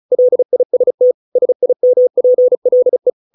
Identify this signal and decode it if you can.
Morse code, heard as a keyed tone, about 34 wpm, 500 Hz. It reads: LIST SIMPLE